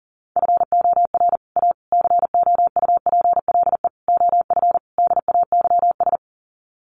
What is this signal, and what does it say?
Morse code, heard as a keyed tone, about 40 wpm, 700 Hz. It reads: FOR A COUPLE OF DAYS
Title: Sets of 5 Words Encore2 40wpm